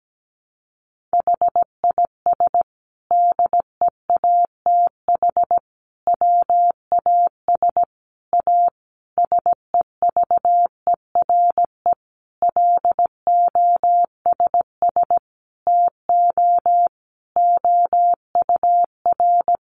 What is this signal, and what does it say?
Morse code, heard as a keyed tone, about 17 wpm, 700 Hz. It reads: HIS DEATH WAS A SEVERE LOSS TO OUR
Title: Sentences from Top 1400 Words 17wpm